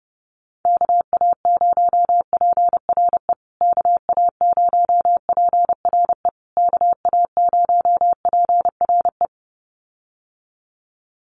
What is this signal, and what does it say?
Morse code, heard as a keyed tone, about 30 wpm, 700 Hz. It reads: KA0PRE KA0PRE KA0PRE
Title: USA Call Signs Repeated 3x 30wpm